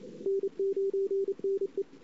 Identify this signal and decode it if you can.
Morse code, heard as a keyed tone, about 32 wpm, 410 Hz. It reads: N9NE
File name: CW / 2008-ss-cw-k5tr-op-n5rz -23:00